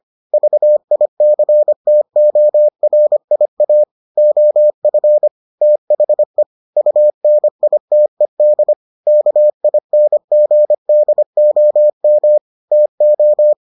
Morse code, heard as a keyed tone, about 25 wpm, 600 Hz. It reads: VICTORIA OF THE UNITED KINGDOM TO